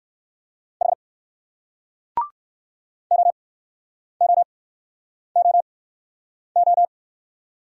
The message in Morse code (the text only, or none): H OOOO